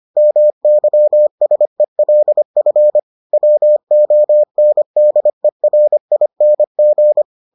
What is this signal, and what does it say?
Morse code, heard as a keyed tone, about 25 wpm, 600 Hz. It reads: MYSELF WONDERING